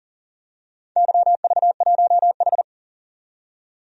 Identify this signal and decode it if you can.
Morse code, heard as a keyed tone, about 42 wpm, 700 Hz. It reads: YV1H
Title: DXCC-100 Call Signs - 4 Characters - 40wpm